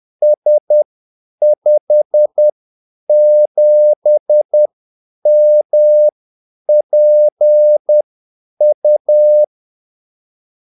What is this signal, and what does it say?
Morse code, heard as a keyed tone, about 10 wpm, spaced as for 8 wpm, 600 Hz. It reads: S57MPU
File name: Callsigns-10-8wpm